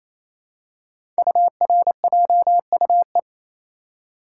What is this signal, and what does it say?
Morse code, heard as a keyed tone, about 27 wpm, 700 Hz. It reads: URJUE
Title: Code Groups - 5 Letters - 28wpm